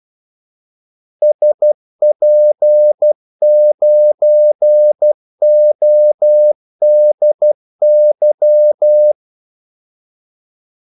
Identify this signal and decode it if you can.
Morse code, heard as a keyed tone, about 12 wpm, 600 Hz. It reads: SP9ODY
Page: 2 ekstra ordmellemrum mellem kaldesignalerne